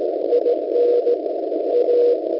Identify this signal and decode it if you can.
Morse code, heard as a keyed tone, about 17 wpm, 480 Hz, with noise. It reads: IN A